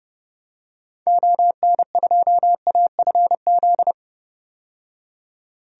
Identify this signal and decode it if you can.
Morse code, heard as a keyed tone, about 30 wpm, 700 Hz. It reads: ON2AFZ
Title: DXCC-100 Call Signs - 6 Characters - 30wpm